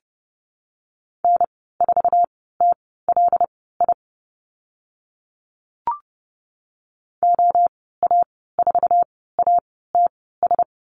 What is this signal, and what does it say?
Morse code, heard as a keyed tone, about 31 wpm, spaced as for 17 wpm, 690 Hz. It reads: N4TLI OA4ATS
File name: Lesson 15g - Number 5 - Call Signs 30wpm